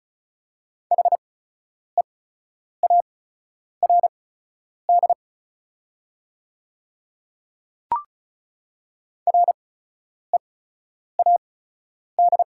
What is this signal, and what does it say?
Morse code, heard as a keyed tone, about 36 wpm, spaced as for 10 wpm, 690 Hz. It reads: HEARD READ
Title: 8x Character Spacing - Top 500 Words - 35wpm